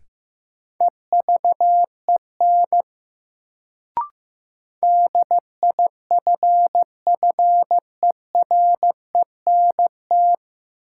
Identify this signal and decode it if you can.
Morse code, heard as a keyed tone, about 15 wpm, 700 Hz. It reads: EVEN DIFFERENT
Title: Sentences from Top 600 Words - New Word Drill - 15wpm